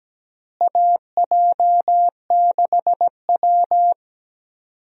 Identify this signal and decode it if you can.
Morse code, heard as a keyed tone, about 17 wpm, 700 Hz. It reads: AJ6W
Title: USA Call Signs - 4 Characters - Repeated 2x - 17wpm